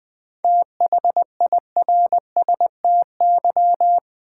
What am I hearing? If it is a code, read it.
Morse code, heard as a keyed tone, about 20 wpm, 700 Hz. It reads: THIRSTY
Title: 7 Letter Words - First 3 Letters Given - 20wpm